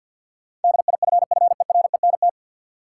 Morse code, heard as a keyed tone, about 45 wpm, 700 Hz. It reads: DIFFERENT
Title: Sentences from Top 600 Words - New Word Drill - 50wpm